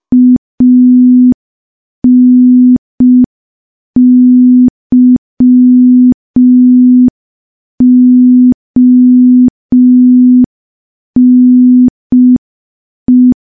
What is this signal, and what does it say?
Morse code, heard as a keyed tone, about 5 wpm, 250 Hz. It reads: ANYONE
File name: morse / morse-training-05WPM